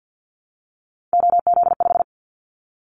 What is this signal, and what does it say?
Morse code, heard as a keyed tone, about 50 wpm, 690 Hz. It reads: OZ5
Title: Callsign Prefixes for Top 100 DXCC - 50wpm